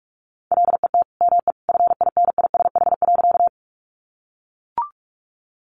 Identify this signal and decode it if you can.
Morse code, heard as a keyed tone, about 44 wpm, 700 Hz. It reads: LET ME FINISH.
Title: Sentences from Top 1200 Words 45wpm